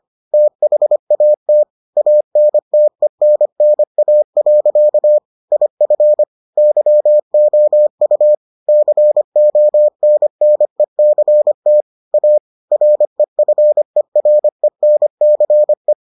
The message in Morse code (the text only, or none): THAT ANTENNA. IF YOU CONNECT A REFERENCE